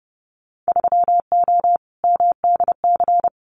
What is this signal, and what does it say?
Morse code, heard as a keyed tone, about 30 wpm, 700 Hz. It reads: 3O MDC
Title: Winter Field Day Exchanges 30wpm